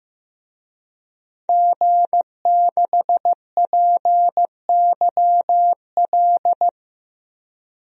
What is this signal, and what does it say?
Morse code, heard as a keyed tone, about 16 wpm, 700 Hz. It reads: G6PYL
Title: DXCC-100 Call Signs - 5 Characters - Repeated 2x - 15wpm